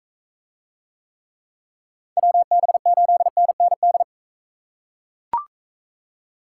Morse code, heard as a keyed tone, about 41 wpm, 700 Hz. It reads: WB8NND